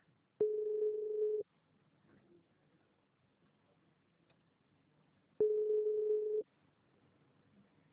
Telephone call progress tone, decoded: ringback tone